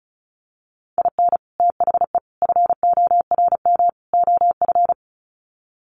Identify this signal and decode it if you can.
Morse code, heard as a keyed tone, about 34 wpm, 700 Hz. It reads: IN THE FORM OF
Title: Sets of 4 Words 35wpm